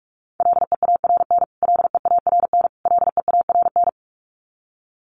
Morse code, heard as a keyed tone, about 46 wpm, 700 Hz. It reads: LEARN LEARN LEARN
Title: Top 300 Words Repeated 45wpm